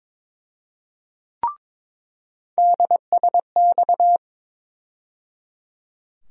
Morse code, heard as a keyed tone, about 22 wpm, 700 Hz. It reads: DSX
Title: USA Call Sign Postfixes - 3 Letters 22wpm